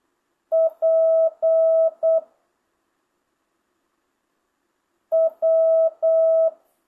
Morse code, heard as a keyed tone, about 8 wpm, 650 Hz. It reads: P W
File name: jh8cbh / 2025cw